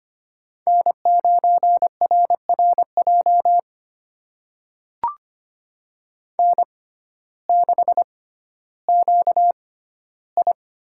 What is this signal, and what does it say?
Morse code, heard as a keyed tone, about 25 wpm, spaced as for 14 wpm, 700 Hz. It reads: N9RRJ N6QI
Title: USA Call Signs - 6x Character Spacing - 25wpm